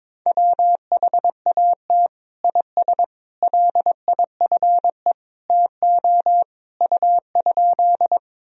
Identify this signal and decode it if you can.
Morse code, heard as a keyed tone, about 21 wpm, 700 Hz. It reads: WHAT IS LIFE TO U?
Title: Sentences from Top 300 Words and Abbreviations 22wpm